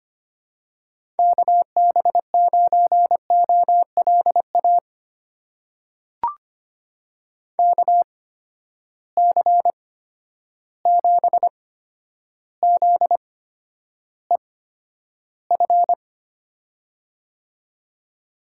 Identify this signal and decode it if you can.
Morse code, heard as a keyed tone, about 25 wpm, spaced as for 11 wpm, 700 Hz. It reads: KB9OLA KC7ZEF
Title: USA Call Signs - 6 Characters - 8x Character Spacing - 25wpm